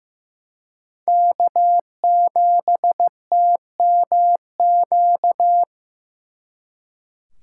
Morse code, heard as a keyed tone, about 18 wpm, 700 Hz. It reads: K7TMQ
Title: Speed Racing - USA 1x3 Call Signs 15wpm